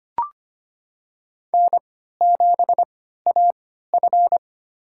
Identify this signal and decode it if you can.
Morse code, heard as a keyed tone, about 25 wpm, spaced as for 14 wpm, 700 Hz. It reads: N7AF
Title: USA Call Signs - 4 Characters - 3x Character Spacing - 25wpm